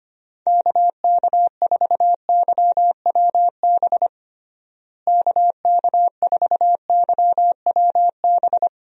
Morse code, heard as a keyed tone, about 25 wpm, 700 Hz. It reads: KK4YWB KK4YWB
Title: USA Call Signs - 6 Characters - Repeated 3x - 25wpm